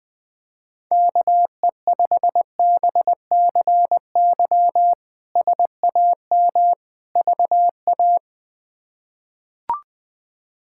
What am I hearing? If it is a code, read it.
Morse code, heard as a keyed tone, about 20 wpm, 700 Hz. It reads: KE5BCY SAM VA
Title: K1USN SST QSO Practice - 20wpm